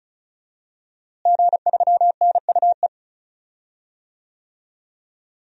Morse code, heard as a keyed tone, about 34 wpm, 700 Hz. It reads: G3NUE
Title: DXCC-100 Call Signs - 5 Characters - Repeated 3x - 35wpm